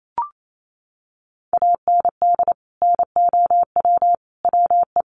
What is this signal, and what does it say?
Morse code, heard as a keyed tone, about 28 wpm, 700 Hz. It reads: AND NOW WE
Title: Sentences from Top 1200 Words 28wpm